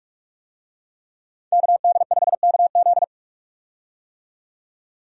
Morse code, heard as a keyed tone, about 43 wpm, 700 Hz. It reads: KD5KZ
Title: USA Call Signs - 5 Characters - 45wpm